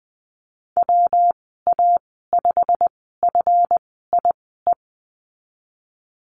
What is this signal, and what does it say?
Morse code, heard as a keyed tone, about 20 wpm, spaced as for 14 wpm, 700 Hz. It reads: WA5FIE
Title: 167 Lesson 27g - Number 7 - Call Signs 20wpm